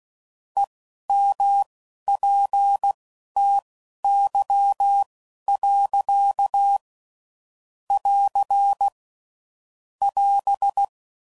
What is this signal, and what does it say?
Morse code, heard as a keyed tone, about 16 wpm, spaced as for 11 wpm, 800 Hz. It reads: EMPTY. <AR> <AS>